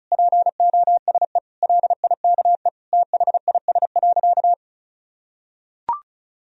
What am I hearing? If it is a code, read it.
Morse code, heard as a keyed tone, about 34 wpm, 700 Hz. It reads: POSE LIKE THIS.
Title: Sentences from Top 1000 Words - Review Part 2 - 35wpm